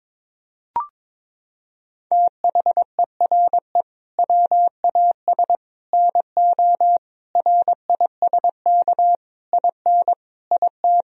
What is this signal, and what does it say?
Morse code, heard as a keyed tone, about 22 wpm, 700 Hz. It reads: THERE WAS NO RISK IN IT
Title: Sentences from Top 1000 Words - Review Part 4 - 22wpm